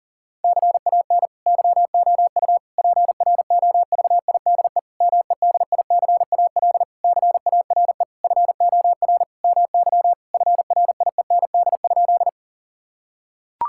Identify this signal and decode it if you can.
Morse code, heard as a keyed tone, about 40 wpm, 700 Hz. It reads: CAN YOU PROVIDE MEDICAL CARE FOR MY FRIEND?
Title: Sentences from Top 1000 Words - Review Part 5 - 40wpm